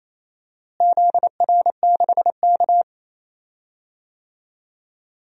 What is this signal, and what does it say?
Morse code, heard as a keyed tone, about 28 wpm, 700 Hz. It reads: ZR6K
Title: DXCC-100 Call Signs - 4 Characters - Repeated 3x - 28wpm